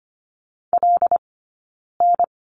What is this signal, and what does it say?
Morse code, heard as a keyed tone, about 25 wpm, 700 Hz. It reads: L N
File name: Lesson 15d - Number 5 - Two Characters 25wpm